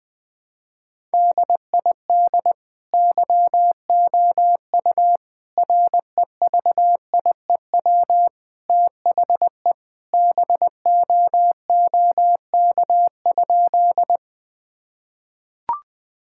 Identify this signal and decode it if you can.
Morse code, heard as a keyed tone, about 20 wpm, 700 Hz. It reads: DID YOU REVIEW THE BOOK?
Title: Sentences from Top 1000 Words - Review Part 6 - 20wpm